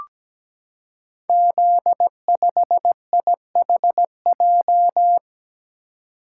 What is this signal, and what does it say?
Morse code, heard as a keyed tone, about 17 wpm, 700 Hz. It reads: Z5IHJ
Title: Code Groups - 5 Letter:Numbers - 17wpm